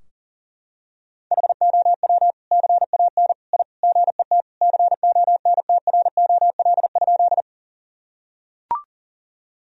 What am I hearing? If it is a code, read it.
Morse code, heard as a keyed tone, about 40 wpm, 700 Hz. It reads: HOW CAN I GET CONTROL?
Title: Sentences from Top 1000 Words - Review Part 3 - 40wpm